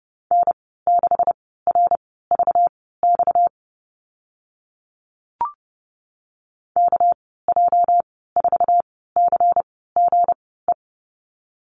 Morse code, heard as a keyed tone, about 30 wpm, spaced as for 17 wpm, 700 Hz. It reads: N6RVX KJ4CGE
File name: Lesson 37g - Letter x - Call Signs 30wpm